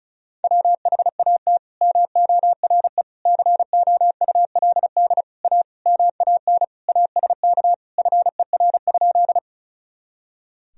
Morse code, heard as a keyed tone, about 36 wpm, 690 Hz. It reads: WHAT MORE COULD A MAN ASK FER?
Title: Sentences from Top 200 Words and Abbreviations 35wpm